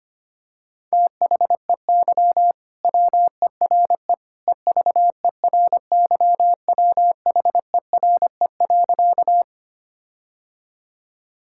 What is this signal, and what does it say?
Morse code, heard as a keyed tone, about 25 wpm, 700 Hz. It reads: THEY WERE EVERYWHERE.
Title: Sentences from Top 1600 Words 25wpm